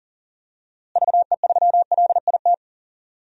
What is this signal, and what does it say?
Morse code, heard as a keyed tone, about 38 wpm, 700 Hz. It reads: VE3LIT
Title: DXCC-100 Call Signs - 6 Characters - Repeated 3x - 40wpm